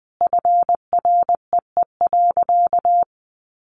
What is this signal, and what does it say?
Morse code, heard as a keyed tone, about 20 wpm, 700 Hz. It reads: FREE.
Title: Additional Easy Phrases 20wpm